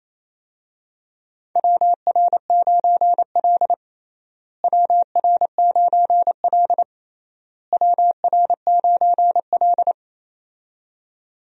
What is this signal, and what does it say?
Morse code, heard as a keyed tone, about 28 wpm, 700 Hz. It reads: WR9L WR9L WR9L
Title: USA Call Signs - 4 Characters - Repeated 3x - 28wpm